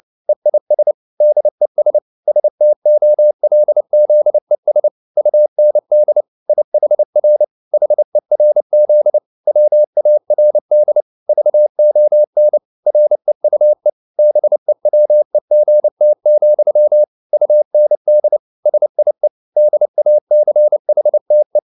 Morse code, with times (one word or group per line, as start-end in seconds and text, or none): 0.3-0.9 EIS
1.2-2.0 DES
2.3-4.9 STOLZES
5.2-6.2 UND
6.5-7.4 IHR
7.7-9.2 HERZ
9.5-11.0 WARD
11.3-12.6 VON
12.9-13.9 REUE
14.2-17.0 BEWEGT,
17.3-18.4 UND
18.7-19.3 SIE
19.6-21.6 DACHTE